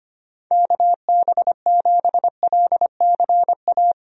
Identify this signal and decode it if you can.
Morse code, heard as a keyed tone, about 25 wpm, 700 Hz. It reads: KB7LCA
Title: USA Call Signs - 6 Characters - Repeated 2x - 25wpm